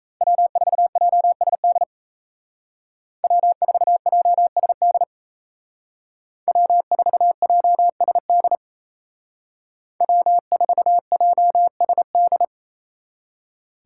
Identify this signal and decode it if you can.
Morse code, heard as a keyed tone, about 36 wpm, 700 Hz. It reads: W4JSD W4JSD W4JSD W4JSD